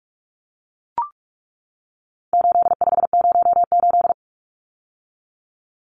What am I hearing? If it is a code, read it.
Morse code, heard as a keyed tone, about 44 wpm, 700 Hz. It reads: 8508
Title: Numbers - 4-Digit 45wpm